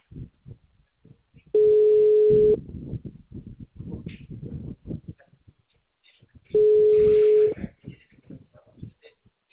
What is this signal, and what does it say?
Ringback tone (Europe, 425 Hz).